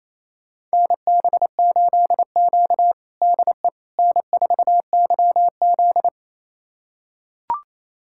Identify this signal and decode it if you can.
Morse code, heard as a keyed tone, about 28 wpm, 700 Hz. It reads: NB8Q DE N4YZ
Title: USA Callsign DE Callsign - 4 Characters - 28wpm